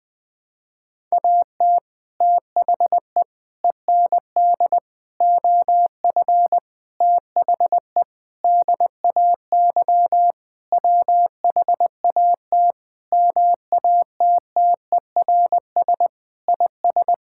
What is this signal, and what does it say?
Morse code, heard as a keyed tone, about 20 wpm, 700 Hz. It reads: AT THE END OF THE DAY WHAT MATTERS IS
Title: Sentences with 6-Word Phrases - 20wpm